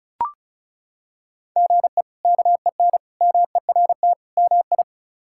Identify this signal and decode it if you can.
Morse code, heard as a keyed tone, about 36 wpm, 700 Hz. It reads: GE KEN MERT MI
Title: K1USN SST QSO Practice - 35wpm